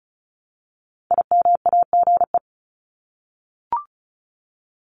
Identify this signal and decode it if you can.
Morse code, heard as a keyed tone, about 34 wpm, 700 Hz. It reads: IMAGE
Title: Top 501-750 Words 35wpm